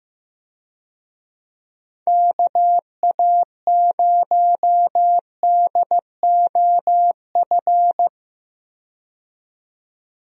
Morse code, heard as a keyed tone, about 15 wpm, 700 Hz. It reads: KA0DOF
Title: USA Call Signs - 6 Characters - 15wpm